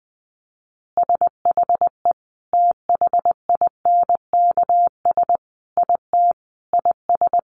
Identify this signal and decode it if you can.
Morse code, heard as a keyed tone, about 20 wpm, 700 Hz. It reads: SHE THINKS IT IS
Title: Easy Phrases 20wpm